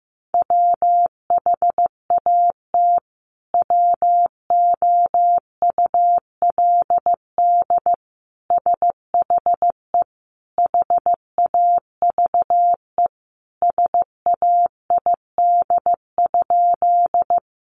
Morse code, heard as a keyed tone, about 15 wpm, 700 Hz. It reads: WHAT WOULD SHE HAVE SAID?
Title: Sentences from Top 100 Words 15wpm